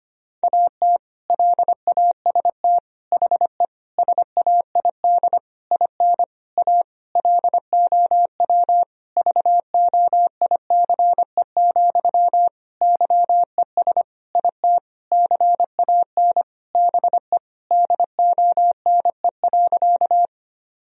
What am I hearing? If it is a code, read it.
Morse code, heard as a keyed tone, about 25 wpm, 700 Hz. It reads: AT LAST HE SAID IN A LOW VOICE, YES IT CAN BE DONE.
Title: Sentences from Top 1000 Words - Review Part 1 - 25wpm